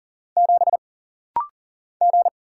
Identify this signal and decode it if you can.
Morse code, heard as a keyed tone, about 39 wpm, 700 Hz. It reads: 7 G
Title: Single Letter-Number - Rapid-Fire 40wpm